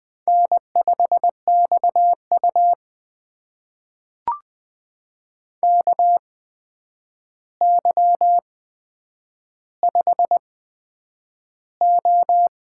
Morse code, heard as a keyed tone, about 20 wpm, spaced as for 9 wpm, 700 Hz. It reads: N5XU KY5O
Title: USA Call Signs - 4 Characters - 8x Character Spacing - 20wpm